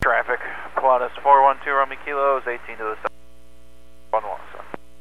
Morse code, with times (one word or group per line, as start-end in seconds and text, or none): none